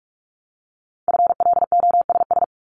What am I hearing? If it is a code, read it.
Morse code, heard as a keyed tone, about 46 wpm, 700 Hz. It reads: FLOSS
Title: Multi-letter Phonemes 45wpm